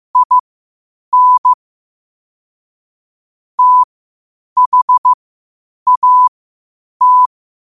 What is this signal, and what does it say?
Morse code, heard as a keyed tone, about 15 wpm, spaced as for 8 wpm, 1000 Hz. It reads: IN THAT
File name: 8wpm